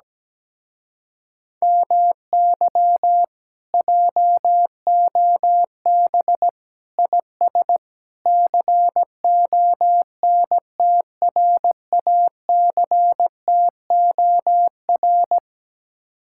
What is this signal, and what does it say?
Morse code, heard as a keyed tone, about 17 wpm, 700 Hz. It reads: MY JOB IS CONTRACTOR
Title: Speed-Racing Common QSO Phrases Encore - 17wpm